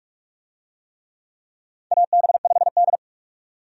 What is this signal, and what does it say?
Morse code, heard as a keyed tone, about 41 wpm, 700 Hz. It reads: AB5D
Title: USA Call Signs - 4 Characters - 45wpm